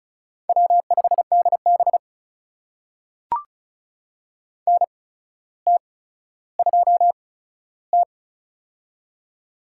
Morse code, heard as a keyed tone, about 36 wpm, spaced as for 15 wpm, 700 Hz. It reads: W5DB NT2T
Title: USA Call Signs - 4 Characters - 8x Character Spacing - 35wpm